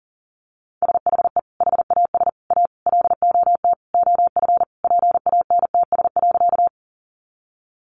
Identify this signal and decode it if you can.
Morse code, heard as a keyed tone, about 40 wpm, 700 Hz. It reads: SHE HAS A LOT OF PANTS.
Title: Additional Easy Phrases 40wpm